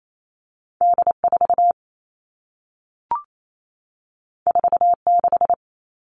Morse code, heard as a keyed tone, about 28 wpm, 700 Hz. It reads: D4 46
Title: All Cut Numbers - 2-Digit 28wpm